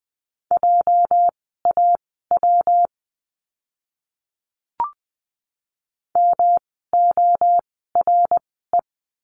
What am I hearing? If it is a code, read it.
Morse code, heard as a keyed tone, about 20 wpm, spaced as for 15 wpm, 700 Hz. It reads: JAW MORE
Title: 206 Lesson 33f - Letter j - Words 20wpm